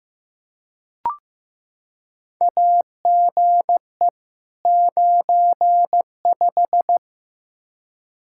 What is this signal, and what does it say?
Morse code, heard as a keyed tone, about 15 wpm, 700 Hz. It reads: AGE 95
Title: K1USN SST QSO Practice 15wpm